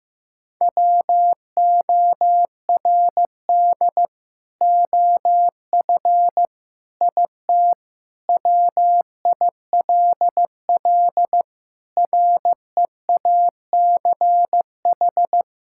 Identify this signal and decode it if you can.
Morse code, heard as a keyed tone, about 16 wpm, 700 Hz. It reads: WORD OF IT WILL REACH
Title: Sentences from Top 1600 Words 15wpm